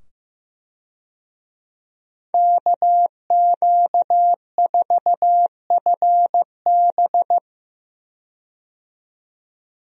Morse code, heard as a keyed tone, about 15 wpm, 700 Hz. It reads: KQ4FB
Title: USA Call Signs - 5 Characters - 15wpm